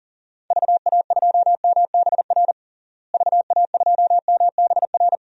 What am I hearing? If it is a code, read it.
Morse code, heard as a keyed tone, about 40 wpm, 700 Hz. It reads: VA2MBR VA2MBR
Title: DXCC-100 Call Signs - 6 Characters - Repeated 3x - 40wpm